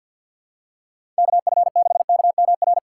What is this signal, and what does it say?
Morse code, heard as a keyed tone, about 47 wpm, 700 Hz. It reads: KF6KMR
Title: USA Call Signs - 6x Character Spacing - 50wpm